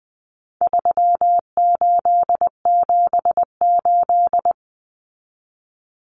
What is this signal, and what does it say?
Morse code, heard as a keyed tone, about 20 wpm, 700 Hz. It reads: 3878
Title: Numbers - 4-Digit 20wpm